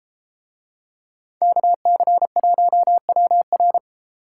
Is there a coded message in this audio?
Yes: Morse code from 1.4 to 3.8 s, heard as a keyed tone, about 30 wpm, 700 Hz. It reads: KC1WR